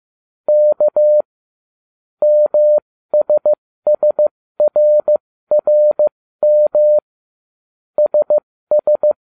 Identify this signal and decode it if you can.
Morse code, heard as a keyed tone, about 15 wpm, spaced as for 13 wpm, 600 Hz. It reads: K MSSRRM SS